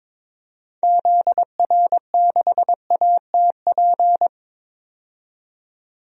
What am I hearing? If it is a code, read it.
Morse code, heard as a keyed tone, about 22 wpm, 700 Hz. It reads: ZR6ATP
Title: DXCC-100 Call Signs - 6 Characters - 22wpm